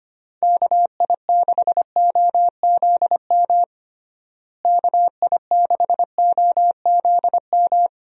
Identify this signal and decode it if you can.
Morse code, heard as a keyed tone, about 25 wpm, 700 Hz. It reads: KI6OZM KI6OZM
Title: USA Call Signs - 6 Characters - Repeated 3x - 25wpm